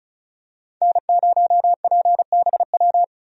Morse code, heard as a keyed tone, about 34 wpm, 700 Hz. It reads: N0PBW
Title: USA Call Signs - 5 Characters - Repeated 2x - 35wpm